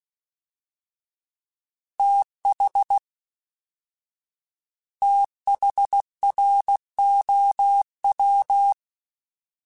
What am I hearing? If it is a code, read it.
Morse code, heard as a keyed tone, about 16 wpm, 800 Hz. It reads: TH THROW
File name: TH / TH mp3s / TH Prefixes 3x2 @13